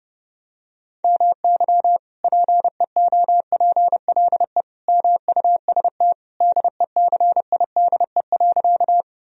Morse code, heard as a keyed tone, about 30 wpm, 700 Hz. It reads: MY PEOPLE MUST DECIDE.
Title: Sentences from Top 1000 Words - Review Part 3 - 30wpm